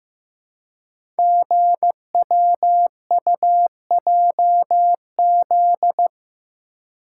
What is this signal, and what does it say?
Morse code, heard as a keyed tone, about 15 wpm, 700 Hz. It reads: GWUJZ
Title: Code Groups - 5 Letters - 15wpm